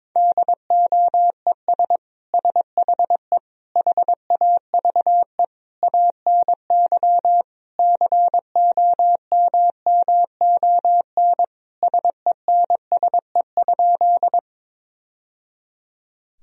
Morse code, heard as a keyed tone, about 22 wpm, 700 Hz. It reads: DOES SHE HAVE ANY COMMON SENSE?
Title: Sentences from Top 1000 Words - Review Part 4 - 22wpm